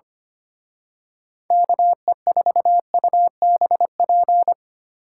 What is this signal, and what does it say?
Morse code, heard as a keyed tone, about 28 wpm, 700 Hz. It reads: KE4UBP
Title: Speed Racing - USA 2x3 Call Signs 25wpm